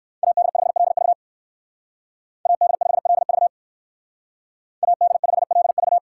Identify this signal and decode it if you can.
Morse code, heard as a keyed tone, about 58 wpm, 700 Hz. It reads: AD5LV AD5LV AD5LV